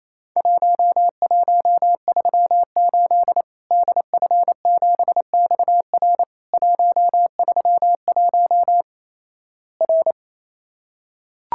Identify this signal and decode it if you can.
Morse code, heard as a keyed tone, about 28 wpm, 700 Hz. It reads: 1138 DF7XR 131 R
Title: Worked All Europe DX Contest - QTC Exchange - 28wpm